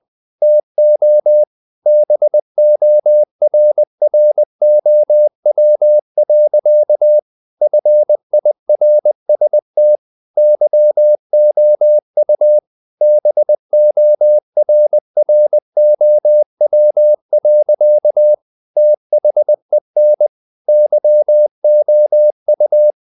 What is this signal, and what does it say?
Morse code, heard as a keyed tone, about 20 wpm, 600 Hz. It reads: TO BORROW. FIRST YOU BORROW. THEN YOU
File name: the old man and the sea / the old man and the sea part 1 20wpm